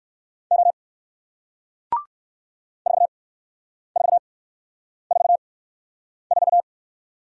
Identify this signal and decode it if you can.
Morse code, heard as a keyed tone, about 53 wpm, 700 Hz. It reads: O 4 4 4 4
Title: Speed-Racing Single Letter-Number 45wpm